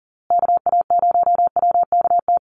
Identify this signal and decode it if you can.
Morse code, heard as a keyed tone, about 40 wpm, 700 Hz. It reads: KA0WKT
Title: USA Call Signs Repeated 2x 40wpm